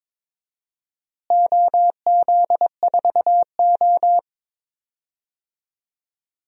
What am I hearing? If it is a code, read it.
Morse code, heard as a keyed tone, about 22 wpm, 700 Hz. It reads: OZ4O
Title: DXCC-100 Call Signs - 4 Characters - Repeated 2x - 22wpm